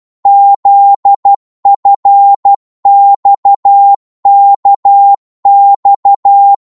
Morse code, heard as a keyed tone, about 12 wpm, 800 Hz. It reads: ZFXKX
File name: Morse / ltr-12wpm